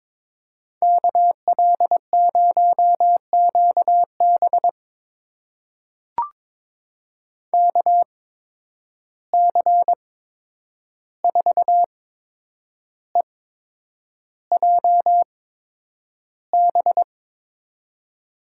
Morse code, heard as a keyed tone, about 21 wpm, spaced as for 9 wpm, 700 Hz. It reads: KL0QB KC4EJB
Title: USA Call Signs - 8x Character Spacing - 22wpm